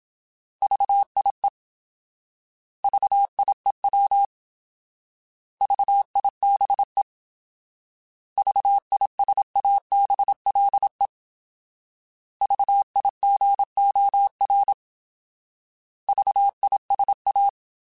Morse code, heard as a keyed tone, about 27 wpm, 800 Hz. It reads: VIE VIEW VIBE VISABLE VIGOR VISA